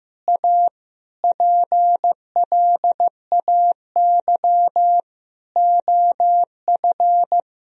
Morse code, heard as a keyed tone, about 15 wpm, 700 Hz. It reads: A PLAY OF
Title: Sentences from Top 1000 Words - Review Part 4 - 15wpm